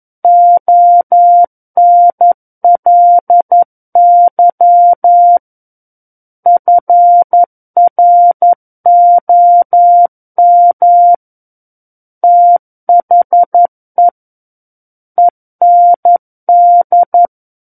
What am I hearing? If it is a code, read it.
Morse code, heard as a keyed tone, about 11 wpm, 700 Hz. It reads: ONLY FROM THE END
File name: Book One / 02-Chapter 2 - 11 WPM